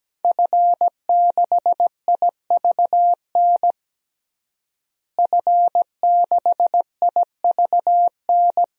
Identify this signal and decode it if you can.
Morse code, heard as a keyed tone, about 17 wpm, 700 Hz. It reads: F6IVN F6IVN
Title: DXCC-100 Call Signs - 5 Characters - Repeated 3x - 17wpm